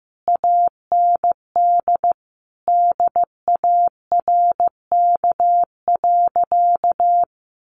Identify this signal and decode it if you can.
Morse code, heard as a keyed tone, about 15 wpm, 700 Hz. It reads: AND DARK.
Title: Sentences from Top 1400 Words 15wpm